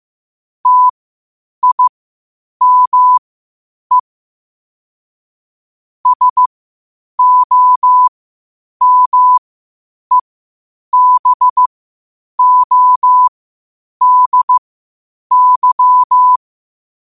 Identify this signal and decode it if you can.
Morse code, heard as a keyed tone, about 15 wpm, spaced as for 8 wpm, 1000 Hz. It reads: TIME SOMEBODY
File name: qotdfiles / 8wpm